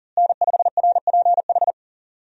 Morse code, heard as a keyed tone, about 40 wpm, 700 Hz. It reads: N5RPH
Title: USA Call Signs - 5 Characters - Repeated 2x - 40wpm